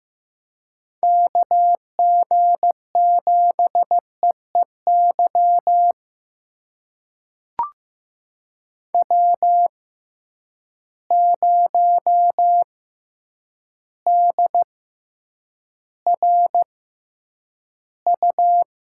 Morse code, heard as a keyed tone, about 15 wpm, spaced as for 8 wpm, 700 Hz. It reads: KG7EEY W0DRU
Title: USA Call Signs - 6x Character Spacing - 15wpm